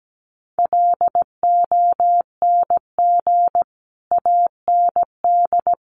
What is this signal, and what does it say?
Morse code, heard as a keyed tone, about 17 wpm, 700 Hz. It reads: LONG AND
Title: Sentences from Top 400 Words 17wpm